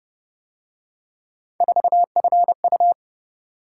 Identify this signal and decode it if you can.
Morse code, heard as a keyed tone, about 31 wpm, 700 Hz. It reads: 4FU